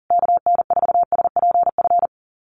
Morse code, heard as a keyed tone, about 40 wpm, 700 Hz. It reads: KN4SPF
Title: USA Call Signs Repeated 3x 40wpm